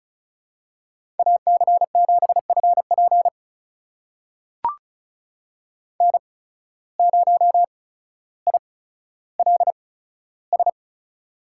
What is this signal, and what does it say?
Morse code, heard as a keyed tone, about 36 wpm, spaced as for 15 wpm, 700 Hz. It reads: AC7FP N0ILS
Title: USA Call Signs - 5 Characters - 8x Character Spacing - 35wpm